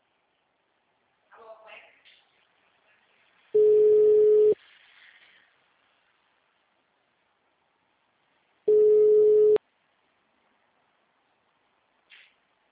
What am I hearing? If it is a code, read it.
Ringback tone (Europe, 425 Hz), with speech.
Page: Americana - SP